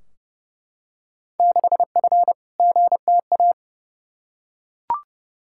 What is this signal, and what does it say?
Morse code, heard as a keyed tone, about 30 wpm, 700 Hz. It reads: 6F GTA
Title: Field Day Exchanges Encore - 30wpm